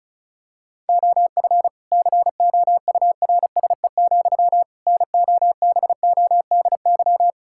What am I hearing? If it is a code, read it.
Morse code, heard as a keyed tone, about 34 wpm, 700 Hz. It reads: OF COURSE, NOBODY
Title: Sentences from Top 1600 Words 35wpm